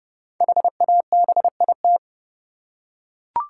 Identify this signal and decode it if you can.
Morse code, heard as a keyed tone, about 31 wpm, 700 Hz. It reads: HABIT
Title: Five Letter Words - 30wpm